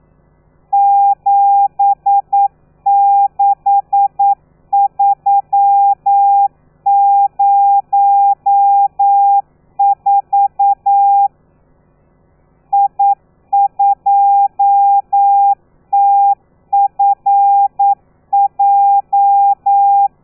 Morse code, heard as a keyed tone, about 9 wpm, 800 Hz. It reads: 76304 I2TFJ
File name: telegrafia / i2tfj / Lez13 40